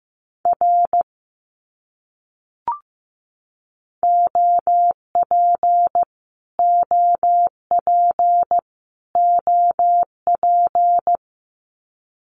Morse code, heard as a keyed tone, about 15 wpm, 700 Hz. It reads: R OP OP OP
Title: Abbreviations Repeated 3x - 15wpm